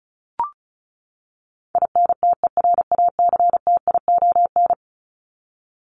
Morse code, heard as a keyed tone, about 34 wpm, 700 Hz. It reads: INTERACTION
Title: Top 1251-1500 Words 35wpm